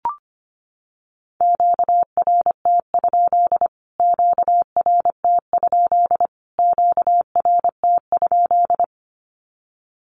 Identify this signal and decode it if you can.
Morse code, heard as a keyed tone, about 25 wpm, 700 Hz. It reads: QRT? QRT? QRT?
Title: Q-Codes Repeated 3x 25wpm